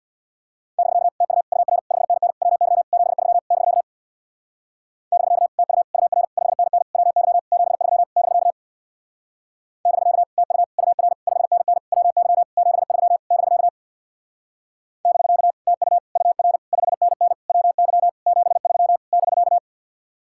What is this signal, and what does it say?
Morse code, heard as a keyed tone, about 64 wpm, 700 Hz. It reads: <BK> TU UR 5NN WY 73 <BK> <BK> TU UR 5NN WY 73 <BK> <BK> TU UR 5NN WY 73 <BK> <BK> TU UR 5NN WY 73 <BK>